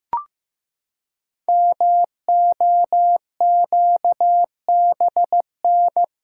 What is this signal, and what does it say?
Morse code, heard as a keyed tone, about 15 wpm, 700 Hz. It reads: MOQBN
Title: Code Groups - 5 Letters - 15wpm